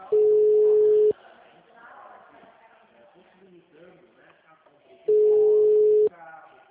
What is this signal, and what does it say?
Ringback tone (Europe, 425 Hz), with speech.
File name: out- -2006- - - .15627